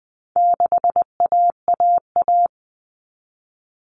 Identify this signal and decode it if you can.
Morse code, heard as a keyed tone, about 20 wpm, 700 Hz. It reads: 6AAA
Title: Common Cut Numbers - 4-Digit 20wpm